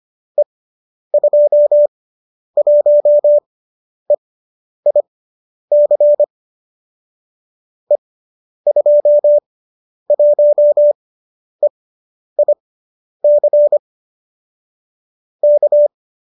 Morse code, heard as a keyed tone, about 25 wpm, spaced as for 10 wpm, 570 Hz. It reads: E21EIC E21EIC K